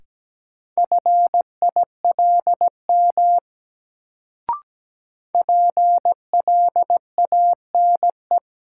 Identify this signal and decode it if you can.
Morse code, heard as a keyed tone, about 17 wpm, 700 Hz. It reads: FILM PLANE
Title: Sentences from Top 1000 Words - Review Part 3 - New Word Drill - 17wpm